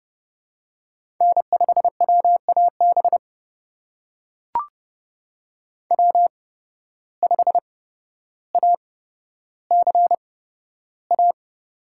Morse code, heard as a keyed tone, about 31 wpm, spaced as for 13 wpm, 700 Hz. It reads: N5WAB W5ACA
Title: USA Call Signs - 5 Characters - 8x Character Spacing - 30wpm